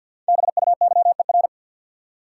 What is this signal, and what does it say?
Morse code, heard as a keyed tone, about 47 wpm, 700 Hz. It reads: BUYER
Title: Five Letter Words - 50wpm